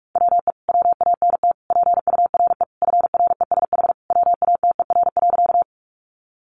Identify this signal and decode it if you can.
Morse code, heard as a keyed tone, about 46 wpm, 700 Hz. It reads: WE WANT PURE FRESH WATER.
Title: Sentences from Top 1500 Words 45wpm